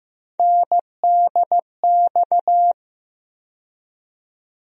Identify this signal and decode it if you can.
Morse code, heard as a keyed tone, about 15 wpm, 700 Hz. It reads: NDX
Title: USA Call Sign Postfixes - 3 Letters 15wpm